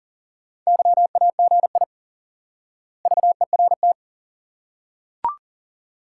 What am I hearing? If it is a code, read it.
Morse code, heard as a keyed tone, about 39 wpm, 700 Hz. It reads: YAGI VERT
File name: IQE Course 3 - Lesson 89 - YAGI - Two Elements - 40wpm